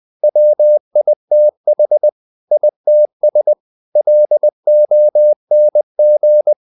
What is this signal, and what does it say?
Morse code, heard as a keyed tone, about 20 wpm, 600 Hz. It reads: WITH ITS LONG
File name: the old man and the sea / the old man and the sea part 1 20wpm